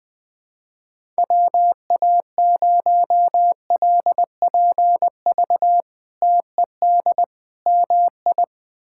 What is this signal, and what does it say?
Morse code, heard as a keyed tone, about 20 wpm, 700 Hz. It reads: WA0LPV TED MI
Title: K1USN SST QSO Practice - 20wpm